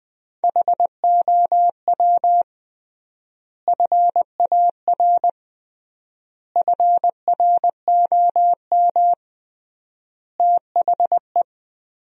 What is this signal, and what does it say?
Morse code, heard as a keyed tone, about 20 wpm, 700 Hz. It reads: HOW FAR FROM THE
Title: 3x Word-Spacing - Sentences from Top 400 Words - 20wpm